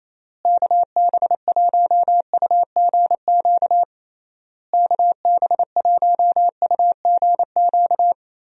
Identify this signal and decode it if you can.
Morse code, heard as a keyed tone, about 28 wpm, 700 Hz. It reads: KB1UGQ KB1UGQ
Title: USA Call Signs - 6 Characters - Repeated 3x - 28wpm